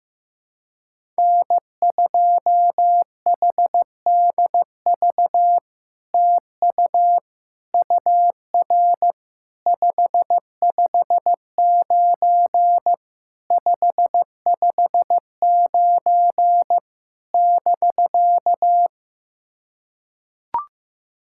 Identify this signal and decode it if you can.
Morse code, heard as a keyed tone, about 15 wpm, 700 Hz. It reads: N2HDV TU UR 559 559 <BK>
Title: SOTA QSO Practice 15wpm